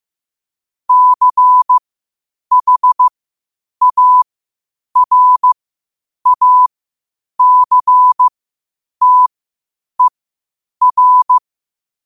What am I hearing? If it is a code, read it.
Morse code, heard as a keyed tone, about 15 wpm, spaced as for 8 wpm, 1000 Hz. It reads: CHARACTER
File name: qotdfiles / 8wpm